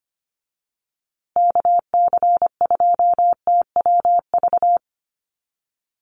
Morse code, heard as a keyed tone, about 25 wpm, 700 Hz. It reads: KC2TWV
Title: USA Call Signs Repeated 3x 25wpm